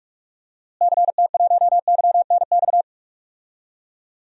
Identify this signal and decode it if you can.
Morse code, heard as a keyed tone, about 46 wpm, 700 Hz. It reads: CT1YNX
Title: DXCC-100 Call Signs - 6 Characters - Repeated 3x - 45wpm